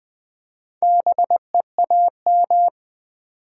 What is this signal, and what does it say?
Morse code, heard as a keyed tone, about 20 wpm, 700 Hz. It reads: BEAM
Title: Top 90 QSO Elements 20wpm